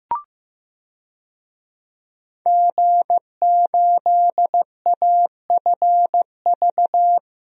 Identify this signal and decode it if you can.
Morse code, heard as a keyed tone, about 15 wpm, 700 Hz. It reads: G8AFV
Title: DXCC-100 Call Signs - 5 Characters - Repeated 2x - 15wpm